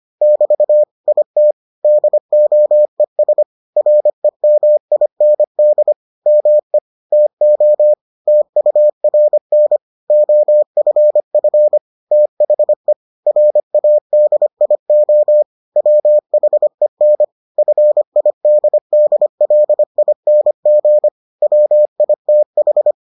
Morse code, heard as a keyed tone, about 25 wpm, 600 Hz. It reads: <BT> IT DOES REMIND ME TO TURN OFF THE RADIO WHEN FIDDLING WITH